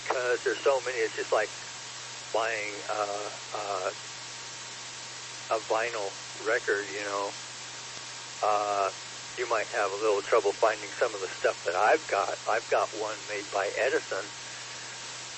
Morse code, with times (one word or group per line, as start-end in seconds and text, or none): none